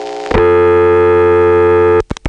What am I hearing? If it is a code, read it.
Dial tone (North America), 350 + 440 Hz, with noise.
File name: phonetrips / PrinceWmCoVA5HQ